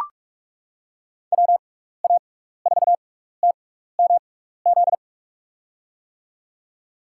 Morse code, heard as a keyed tone, about 44 wpm, spaced as for 15 wpm, 700 Hz. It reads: WA4TMZ
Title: USA Call Signs - 6x Character Spacing - 45wpm